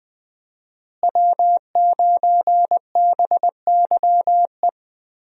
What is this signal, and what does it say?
Morse code, heard as a keyed tone, about 20 wpm, 700 Hz. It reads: W9BYE
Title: USA Call Signs - 5 Characters - Repeated 3x - 20wpm